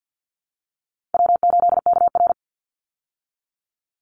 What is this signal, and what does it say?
Morse code, heard as a keyed tone, about 50 wpm, 700 Hz. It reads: W8KR
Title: Call Signs Repeated 3x - 50wpm